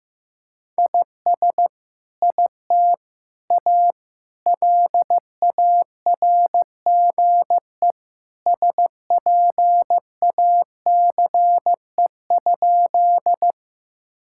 Morse code, heard as a keyed tone, about 15 wpm, 700 Hz. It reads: IS IT A LARGE SPACE?
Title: Sentences from Top 1000 Words - Review Part 5 - 15wpm